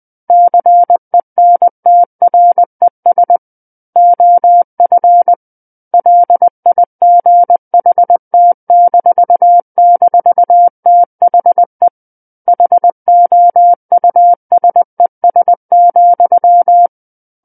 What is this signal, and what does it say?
Morse code, heard as a keyed tone, about 20 wpm, 700 Hz. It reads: CENTRES OF LIGHT--THE HOUSES,
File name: 20 WPM